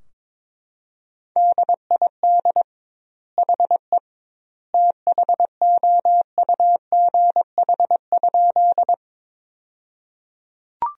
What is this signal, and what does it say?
Morse code, heard as a keyed tone, about 22 wpm, 700 Hz. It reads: DID HE THOUGH?
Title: 2x Word-Spacing - Sentences from Top 400 Words - 22wpm